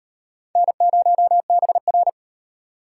Morse code, heard as a keyed tone, about 35 wpm, 700 Hz. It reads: N0BR